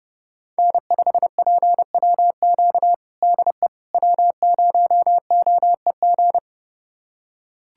Morse code, heard as a keyed tone, about 30 wpm, 700 Hz. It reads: N5PWQ DE W0OEG
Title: USA Callsign DE Callsign - 5 Characters - 30wpm